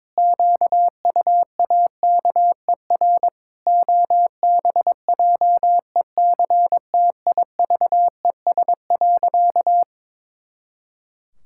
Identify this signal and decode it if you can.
Morse code, heard as a keyed tone, about 22 wpm, 700 Hz. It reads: QUAKER OBJECTIVES.
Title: Pangram Challenge v2 - 22wpm